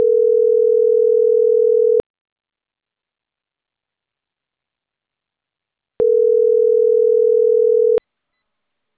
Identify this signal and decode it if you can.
Ringback tone (North America), 440 + 480 Hz.